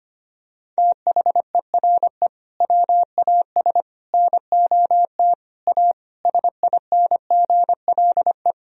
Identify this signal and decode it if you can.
Morse code, heard as a keyed tone, about 25 wpm, 690 Hz. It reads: THERE WAS NOT A SINGLE
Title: Sentences from Top 1000 Words - Review Part 4 - 25wpm